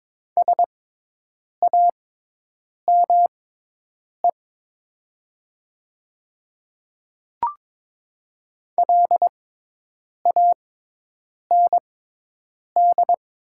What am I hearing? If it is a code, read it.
Morse code, heard as a keyed tone, about 21 wpm, spaced as for 8 wpm, 700 Hz. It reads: SAME LAND
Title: 6x Character Spacing - Top 500 Words - 22wpm